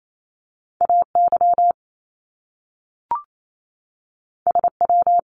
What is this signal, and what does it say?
Morse code, heard as a keyed tone, about 28 wpm, 700 Hz. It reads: AY SW
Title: Top Two Letter Combinations in Words 28wpm